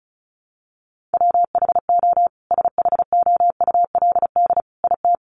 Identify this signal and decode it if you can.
Morse code, heard as a keyed tone, about 34 wpm, 700 Hz. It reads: WHO SHOULD IT
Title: Sentences from Top 300 Words 35wpm